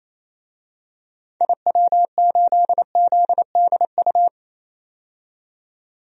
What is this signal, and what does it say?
Morse code, heard as a keyed tone, about 28 wpm, 700 Hz. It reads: IW8ZDU
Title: DXCC-100 Call Signs - 6 Characters - 28wpm